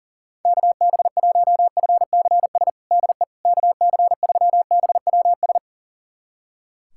Morse code, heard as a keyed tone, about 40 wpm, 700 Hz. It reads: KB1FCS DE KC3BWS
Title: USA Callsign DE Callsign 40wpm